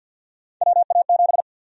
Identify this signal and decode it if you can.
Morse code, heard as a keyed tone, about 52 wpm, 700 Hz. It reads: WA7
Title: USA Call Sign Prefixes - 50wpm